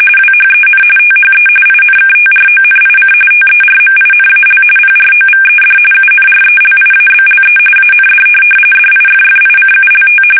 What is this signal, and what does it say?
RTTY, about 45.45 baud. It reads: WHEN THE INKTRONIC TERMINAL PROVED UNRELIABLE AND DIFFICULT
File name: fsk850 / teletype-corp